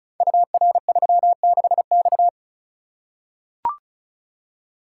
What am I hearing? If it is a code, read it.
Morse code, heard as a keyed tone, about 36 wpm, 700 Hz. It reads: UR36X
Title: Code Groups - 5 Letter:Numbers - 35wpm